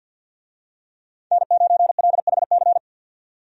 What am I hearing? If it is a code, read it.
Morse code, heard as a keyed tone, about 49 wpm, 700 Hz. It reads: N9LHC
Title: USA Call Signs - 5 Characters - 50wpm